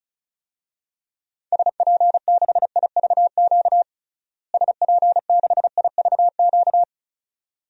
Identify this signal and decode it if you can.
Morse code, heard as a keyed tone, about 36 wpm, 700 Hz. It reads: SP6IVQ SP6IVQ
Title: DXCC-100 Call Signs - 6 Characters - Repeated 2x - 35wpm